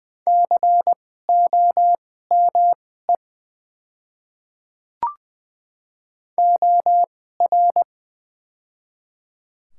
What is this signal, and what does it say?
Morse code, heard as a keyed tone, about 20 wpm, spaced as for 15 wpm, 700 Hz. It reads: COME OR
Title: 2x Character Spacing - Top 100 Words - 20wpm